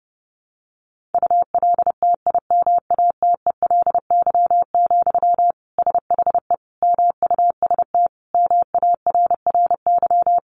Morse code, heard as a keyed tone, about 30 wpm, 700 Hz. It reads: ULTIMATELY, SHE MUST MARRY
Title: Sentences from Top 1200 Words 30wpm